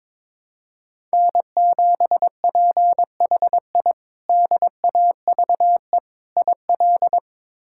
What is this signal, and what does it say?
Morse code, heard as a keyed tone, about 22 wpm, 700 Hz. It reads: N7PHI DAVE IL
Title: K1USN SST QSO Practice - 22wpm